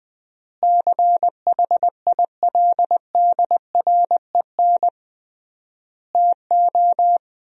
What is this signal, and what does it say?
Morse code, heard as a keyed tone, about 21 wpm, 700 Hz. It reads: CHILDREN TO
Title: 3x Word-Spacing - Sentences from Top 400 Words - 20wpm